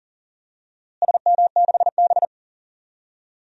Morse code, heard as a keyed tone, about 40 wpm, 700 Hz. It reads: SM6B
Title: DXCC-100 Call Signs - 4 Characters - Repeated 2x - 40wpm